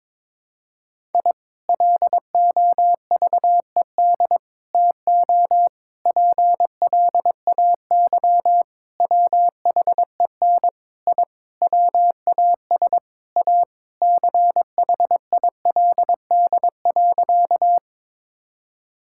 Morse code, heard as a keyed tone, about 22 wpm, 700 Hz. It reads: I LOVED TO PLAY WHEN I WAS A CHILD.
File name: practice / 22wpm / Sentences with 4-Word Phrases - 22wpm